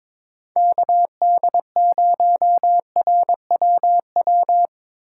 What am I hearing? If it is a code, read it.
Morse code, heard as a keyed tone, about 25 wpm, 700 Hz. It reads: KD0RWW
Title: Speed Racing - USA 2x3 Call Signs 22wpm